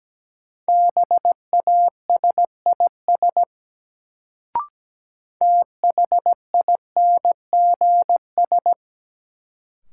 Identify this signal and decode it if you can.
Morse code, heard as a keyed tone, about 17 wpm, 700 Hz. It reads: BASIS THINGS
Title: Sets of 3 Words - New Word Drill - 17wpm